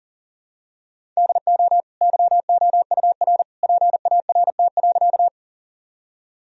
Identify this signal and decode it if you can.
Morse code, heard as a keyed tone, about 37 wpm, 700 Hz. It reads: DO YOUR PART.
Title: Sentences from Top 1000 Words - Review Part 1 - 40wpm